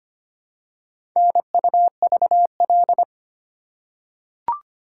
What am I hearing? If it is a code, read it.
Morse code, heard as a keyed tone, about 25 wpm, 700 Hz. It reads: NUVL
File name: practice / 25wpm / Four Letter-Number Combinations - 25wpm